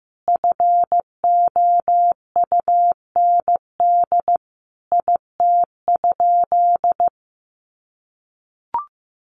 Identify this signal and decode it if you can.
Morse code, heard as a keyed tone, about 15 wpm, 700 Hz. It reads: FOUND IT?
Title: Sentences from Top 300 Words 15wpm